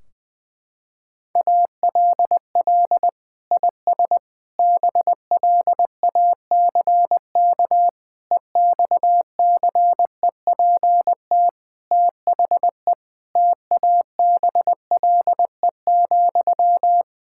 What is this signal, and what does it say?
Morse code, heard as a keyed tone, about 20 wpm, 700 Hz. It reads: ALL IS BLACK EXCEPT THE TABLE,
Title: Sentences from Top 1600 Words 20wpm